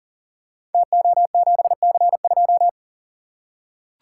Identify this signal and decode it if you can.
Morse code, heard as a keyed tone, about 42 wpm, 690 Hz. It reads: TO7C2
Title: Code Groups - 5 Letter:Numbers - 40wpm